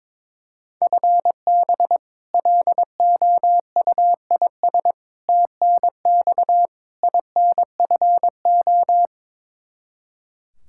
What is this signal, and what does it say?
Morse code, heard as a keyed tone, about 22 wpm, 700 Hz. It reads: FB LOUIS TNX INFO
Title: Practice QSO - Exchange 3 - More Info - 22wpm